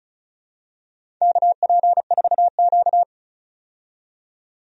Morse code, heard as a keyed tone, about 36 wpm, 700 Hz. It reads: KP4Q
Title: DXCC-100 Call Signs - 4 Characters - 35wpm